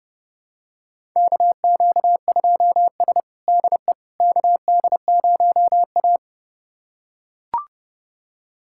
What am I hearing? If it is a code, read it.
Morse code, heard as a keyed tone, about 31 wpm, 700 Hz. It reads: KQ2S DE KD0A
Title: USA Callsign DE Callsign - 4 Characters - 30wpm